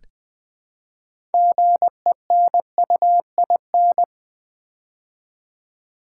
Morse code, heard as a keyed tone, about 20 wpm, 700 Hz. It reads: GENUIN
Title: Build A Word - 3 Letters At A Time - 9-Letter Words - 20wpm